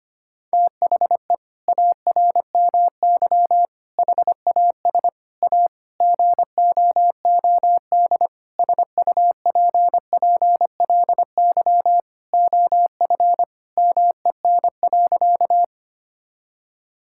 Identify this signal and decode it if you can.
Morse code, heard as a keyed tone, about 25 wpm, 700 Hz. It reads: THE ARMY HAS A GOOD SUPPLY OF MEN.
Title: Sentences from Top 1000 Words - Review Part 5 - 25wpm